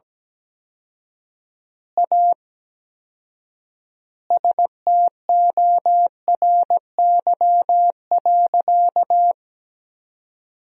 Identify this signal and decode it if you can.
Morse code, heard as a keyed tone, about 17 wpm, 700 Hz. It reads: A STORY.
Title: 4x Word-Spacing - Sentences from Top 300 Words - 17wpm